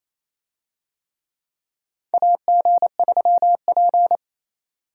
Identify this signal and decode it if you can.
Morse code, heard as a keyed tone, about 27 wpm, 700 Hz. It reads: AG3P
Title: USA Call Signs - 4 Characters - 28wpm